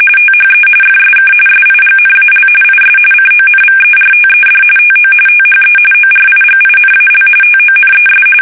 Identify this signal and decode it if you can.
RTTY, about 45.45 baud. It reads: THE FSK SIGNAL IS CONVERTED TO THE ORIGINAL TONES
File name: fsk850 / teletype-RTTY